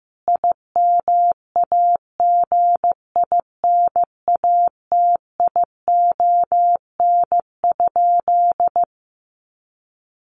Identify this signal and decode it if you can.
Morse code, heard as a keyed tone, about 15 wpm, 700 Hz. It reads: IMAGINATION?
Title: Sentences from Top 1300 Words 15wpm